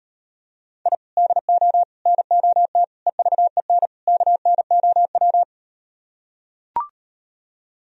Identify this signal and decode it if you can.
Morse code, heard as a keyed tone, about 38 wpm, 700 Hz. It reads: I DO NOT EVEN KNOW